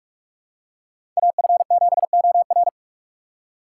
Morse code, heard as a keyed tone, about 46 wpm, 700 Hz. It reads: AF7OR
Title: USA Call Signs - 2x Character Spacing - 45wpm